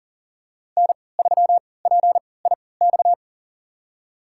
Morse code, heard as a keyed tone, about 40 wpm, spaced as for 23 wpm, 700 Hz. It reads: N3PIX
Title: USA Call Signs - 5 Characters - 3x Character Spacing - 40wpm